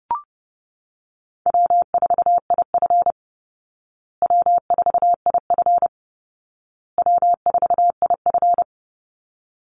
Morse code, heard as a keyed tone, about 30 wpm, 700 Hz. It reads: W4IF W4IF W4IF
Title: Easy Call Signs Repeated 3x - 30wpm